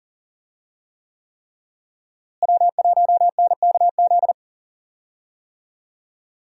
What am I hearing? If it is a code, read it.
Morse code, heard as a keyed tone, about 40 wpm, 700 Hz. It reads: W1NKZ
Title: USA Call Signs - 5 Characters - Repeated 3x - 40wpm